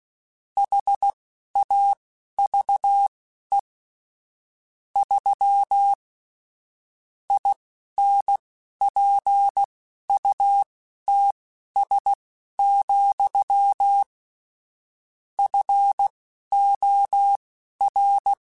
Morse code, heard as a keyed tone, about 16 wpm, spaced as for 11 wpm, 800 Hz. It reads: HAVE 3 INPUTS, FOR
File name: Technical Subjects / Technical Subjects-mp3s / AND Gate @10